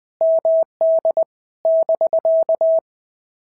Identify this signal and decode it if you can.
Morse code, heard as a keyed tone, about 19 wpm, 650 Hz. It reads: MD <BK>
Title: POTA QSO Practice - 20wpm